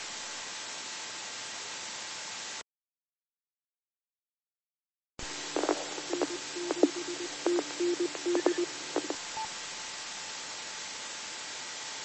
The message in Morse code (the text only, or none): none